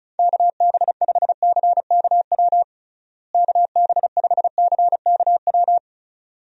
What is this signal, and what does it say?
Morse code, heard as a keyed tone, about 36 wpm, 700 Hz. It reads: KB5CKW KB5CKW
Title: USA Call Signs - 6 Characters - Repeated 2x - 35wpm